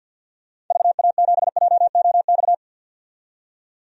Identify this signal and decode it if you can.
Morse code, heard as a keyed tone, about 45 wpm, 700 Hz. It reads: VA7JOX
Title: DXCC-100 Call Signs - 6 Characters - 50wpm